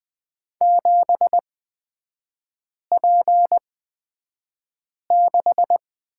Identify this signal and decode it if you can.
Morse code, heard as a keyed tone, about 20 wpm, spaced as for 5 wpm, 700 Hz. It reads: 7P6
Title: 255 Lesson 41e - Period - Three Characters 20wpm